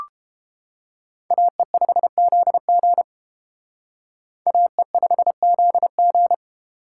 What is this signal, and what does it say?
Morse code, heard as a keyed tone, about 29 wpm, 700 Hz. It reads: AE5ZG AE5ZG